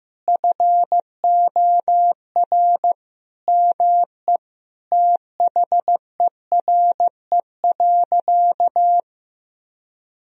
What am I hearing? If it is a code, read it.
Morse code, heard as a keyed tone, about 15 wpm, 700 Hz. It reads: FOR ME THERE.
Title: Sentences from Top 1000 Words - Review Part 4 - 15wpm